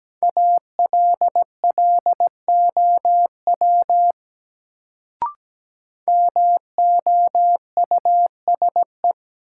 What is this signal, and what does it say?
Morse code, heard as a keyed tone, about 17 wpm, 700 Hz. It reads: ALLOW MOUSE
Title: Sentences from Top 700 Words - New Word Drill - 17wpm